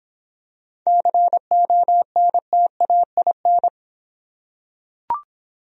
Morse code, heard as a keyed tone, about 26 wpm, 700 Hz. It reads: CONTAIN